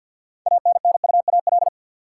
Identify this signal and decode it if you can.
Morse code, heard as a keyed tone, about 50 wpm, 700 Hz. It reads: ANNUAL
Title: Six Letter Words - 50wpm